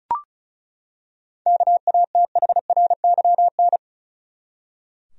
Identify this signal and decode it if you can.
Morse code, heard as a keyed tone, about 34 wpm, 700 Hz. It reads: KATHRYN
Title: USA Common Names - 35wpm